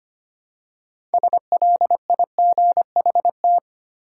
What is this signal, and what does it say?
Morse code, heard as a keyed tone, about 25 wpm, 700 Hz. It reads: SLIGHT
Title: Six Letter Words - 25wpm